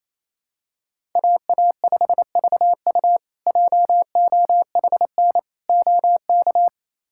Morse code, heard as a keyed tone, about 28 wpm, 700 Hz. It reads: AA5VU JOHN OK
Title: K1USN SST QSO Practice - 28wpm